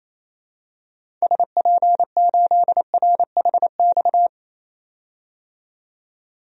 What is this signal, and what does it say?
Morse code, heard as a keyed tone, about 28 wpm, 700 Hz. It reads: SP8RHX
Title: DXCC-100 Call Signs - 6 Characters - 28wpm